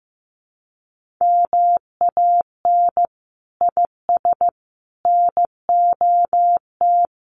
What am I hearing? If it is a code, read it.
Morse code, heard as a keyed tone, about 15 wpm, 700 Hz. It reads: MAN IS NOT
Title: Sentences from Top 1400 Words 15wpm